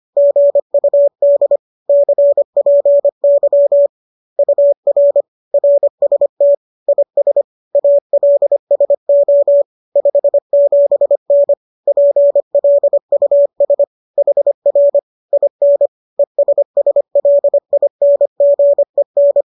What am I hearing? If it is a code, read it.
Morse code, heard as a keyed tone, about 25 wpm, 570 Hz. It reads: GUD CPY UR RST IS ALSO 57N PLUS HR IN ESSLINGEN
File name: qso218 25